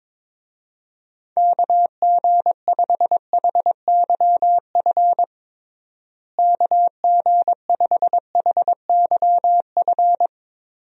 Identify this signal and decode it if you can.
Morse code, heard as a keyed tone, about 22 wpm, 700 Hz. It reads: KG5HYF KG5HYF
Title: USA Call Signs - 6 Characters - Repeated 3x - 22wpm